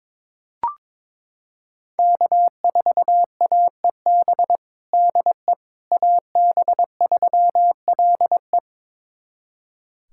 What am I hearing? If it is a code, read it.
Morse code, heard as a keyed tone, about 22 wpm, 700 Hz. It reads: K4AEB DE AB3LE
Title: USA Callsign DE Callsign - 5 Characters - 22wpm